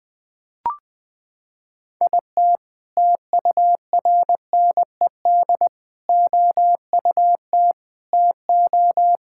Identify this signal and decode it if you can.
Morse code, heard as a keyed tone, about 21 wpm, 700 Hz. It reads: IT TURNED OUT TO
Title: Sentences with 5-Word Phrases - 20wpm